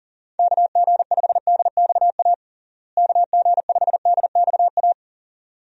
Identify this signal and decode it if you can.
Morse code, heard as a keyed tone, about 40 wpm, 700 Hz. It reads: KG5DXA KG5DXA
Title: USA Call Signs - 6 Characters - Repeated 2x - 40wpm